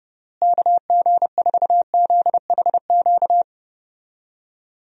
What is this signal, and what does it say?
Morse code, heard as a keyed tone, about 32 wpm, 700 Hz. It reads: KG4ZHQ
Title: USA Call Signs - 6 Characters - Repeated 3x - 30wpm